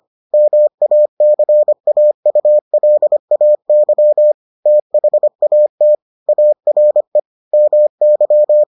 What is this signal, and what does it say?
Morse code, heard as a keyed tone, about 25 wpm, 600 Hz. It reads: MACAULAY THAT ARE MY